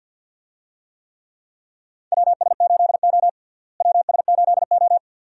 Morse code, heard as a keyed tone, about 51 wpm, 700 Hz. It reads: WS8O WS8O
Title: USA Call Signs - 4 Characters - Repeated 2x - 50wpm